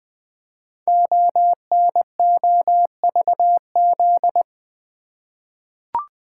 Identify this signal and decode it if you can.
Morse code, heard as a keyed tone, about 20 wpm, 700 Hz. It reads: ONOVZ
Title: Code Groups - 5 Letters - 20wpm